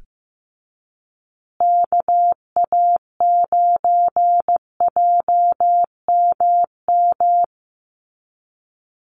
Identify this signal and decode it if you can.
Morse code, heard as a keyed tone, about 15 wpm, 700 Hz. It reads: KA9JMM
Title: USA Call Signs Repeated 2x 15wpm